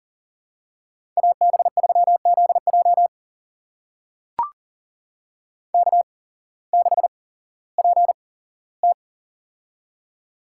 Morse code, heard as a keyed tone, about 40 wpm, spaced as for 19 wpm, 700 Hz. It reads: AB3ZJ K6PT
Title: USA Call Signs - 8x Character Spacing - 40wpm